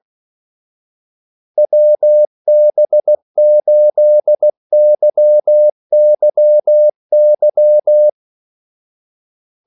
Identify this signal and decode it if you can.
Morse code, heard as a keyed tone, about 16 wpm, 600 Hz. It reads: WB8YYY
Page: Kaldesignaler 16 wpm